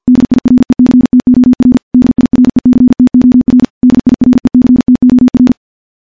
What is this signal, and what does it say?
Morse code, heard as a keyed tone, about 51 wpm, 250 Hz. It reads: DIRECTOR DIRECTOR DIRECTOR
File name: morse / morse-training-50WPM